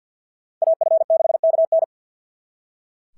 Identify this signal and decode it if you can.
Morse code, heard as a keyed tone, about 46 wpm, 690 Hz. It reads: AF6KN
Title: SOTA QSO Practice 50wpm